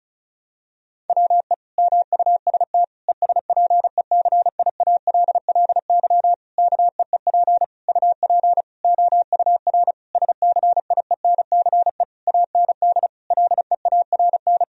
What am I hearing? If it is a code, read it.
Morse code, heard as a keyed tone, about 36 wpm, 700 Hz. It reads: WE MUST ESPECIALLY KEEP UP OUR SCIENCE AND LEARN
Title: Sentences from Top 1000 Words - Review Part 4 - 35wpm